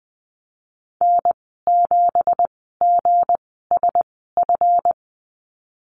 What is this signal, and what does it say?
Morse code, heard as a keyed tone, about 20 wpm, spaced as for 15 wpm, 700 Hz. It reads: N7GSF
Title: 167 Lesson 27g - Number 7 - Call Signs 20wpm